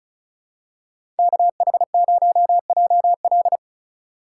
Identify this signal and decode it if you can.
Morse code, heard as a keyed tone, about 36 wpm, 700 Hz. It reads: KH0JL
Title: USA Call Signs - 5 Characters - 2x Character Spacing - 35wpm